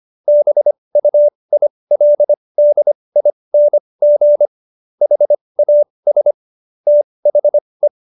Morse code, heard as a keyed tone, about 25 wpm, spaced as for 20 wpm, 590 Hz. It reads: BUILDING HAS THE
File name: Jokes-25-char-20-eff